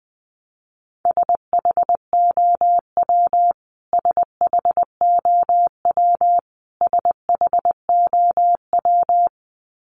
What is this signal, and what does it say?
Morse code, heard as a keyed tone, about 20 wpm, 700 Hz. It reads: SHOW SHOW SHOW
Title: Top 500 Words Repeated 20wpm